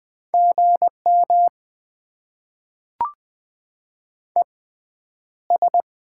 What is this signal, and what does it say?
Morse code, heard as a keyed tone, about 20 wpm, spaced as for 10 wpm, 690 Hz. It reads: GM ES
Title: Top 20 QSO Elements - 6x Character Spacing - 20wpm